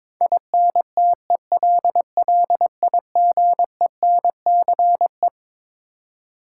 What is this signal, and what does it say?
Morse code, heard as a keyed tone, about 22 wpm, 700 Hz. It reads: INTELLIGENCE
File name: Twelve Letter Words - 22wpm